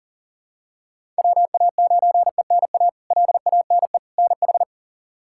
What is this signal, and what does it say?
Morse code, heard as a keyed tone, about 40 wpm, 700 Hz. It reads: WA9ENA LANE NH
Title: K1USN SST QSO Practice - 40wpm